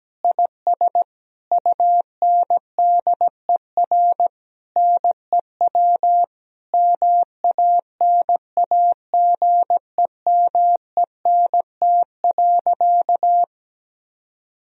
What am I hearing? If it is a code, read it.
Morse code, heard as a keyed tone, about 17 wpm, 700 Hz. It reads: IS UNDER NEW MANAGEMENT.
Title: Sentences from Top 1000 Words - Review Part 4 - 17wpm